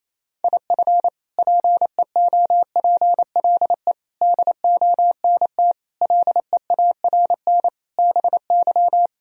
Morse code, heard as a keyed tone, about 28 wpm, 700 Hz. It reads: IF PEOPLE DONT LEARN BY
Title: Sentences from Top 1000 Words - Review Part 3 - 28wpm